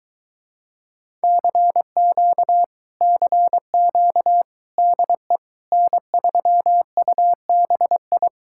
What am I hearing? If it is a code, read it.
Morse code, heard as a keyed tone, about 22 wpm, 700 Hz. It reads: CQ CQ DE N3UBI